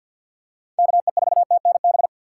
Morse code, heard as a keyed tone, about 52 wpm, 700 Hz. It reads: KE4TNB
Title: USA Call Signs - 4x Character Spacing - 50wpm